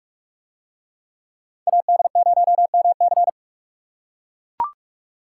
Morse code, heard as a keyed tone, about 42 wpm, 700 Hz. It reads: AD0MC